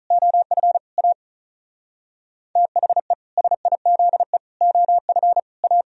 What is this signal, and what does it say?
Morse code, heard as a keyed tone, about 45 wpm, 700 Hz. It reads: OF A THE SIZE OF A